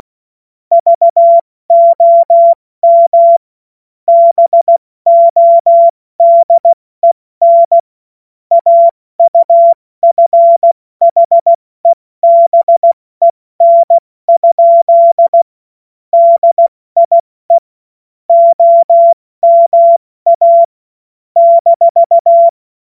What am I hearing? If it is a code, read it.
Morse code, heard as a keyed tone, about 16 wpm, 680 Hz. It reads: VOM BODEN AUFHEBEN? DIE OMA -
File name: Morsekurs / Witze / Witz 17. August 2021, 70 F80